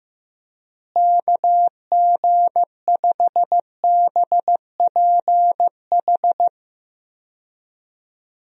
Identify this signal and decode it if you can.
Morse code, heard as a keyed tone, about 15 wpm, 700 Hz. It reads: KG5BPH
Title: USA Call Signs - 6 Characters - Repeated 3x - 15wpm